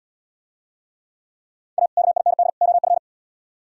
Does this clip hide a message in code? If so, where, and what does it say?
Yes: Morse code from 1.8 to 3.0 s, heard as a keyed tone, about 76 wpm, 700 Hz. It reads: A KIND OF